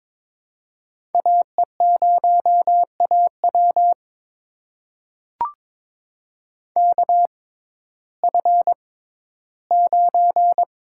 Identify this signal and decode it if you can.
Morse code, heard as a keyed tone, about 22 wpm, spaced as for 13 wpm, 700 Hz. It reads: AE0AW KF9
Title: USA Call Signs - 5 Characters - 6x Character Spacing - 22wpm